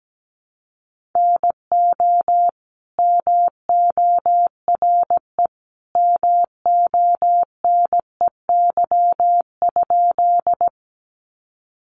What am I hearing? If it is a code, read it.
Morse code, heard as a keyed tone, about 17 wpm, 700 Hz. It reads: NO MORE MONEY?
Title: Sentences from Top 400 Words 17wpm